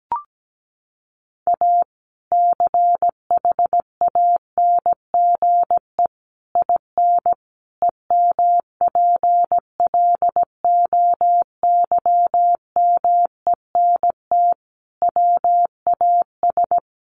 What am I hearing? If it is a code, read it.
Morse code, heard as a keyed tone, about 17 wpm, 700 Hz. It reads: A CHANGE IN EMPLOYMENT WAS
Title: Sentences from Top 1500 Words 17wpm